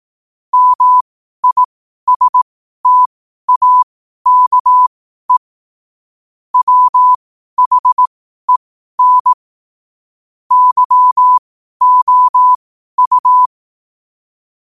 Morse code, heard as a keyed tone, about 18 wpm, spaced as for 12 wpm, 1000 Hz. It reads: MISTAKE WHEN YOU
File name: qotdfiles / 12wpm